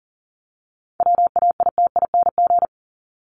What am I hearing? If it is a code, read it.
Morse code, heard as a keyed tone, about 40 wpm, 700 Hz. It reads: WAITING
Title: ING Suffix - Top 100 Words with Base Words 40wpm